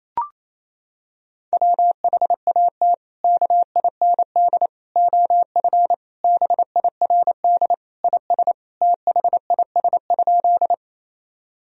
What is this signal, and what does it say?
Morse code, heard as a keyed tone, about 28 wpm, 700 Hz. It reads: WHAT KIND OF BIRD IS THIS?
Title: Sentences from Top 1000 Words - Review Part 2 - 28wpm